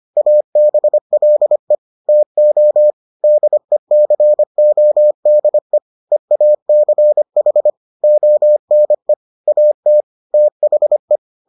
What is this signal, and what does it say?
Morse code, heard as a keyed tone, about 25 wpm, 600 Hz. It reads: ABLE TO DECODE EACH ONE AT THE